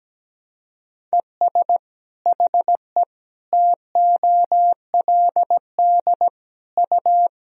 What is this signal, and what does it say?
Morse code, heard as a keyed tone, about 17 wpm, 700 Hz. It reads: ES HE TOLD U
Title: Sentences from Top 500 Words and Abbreviations 17wpm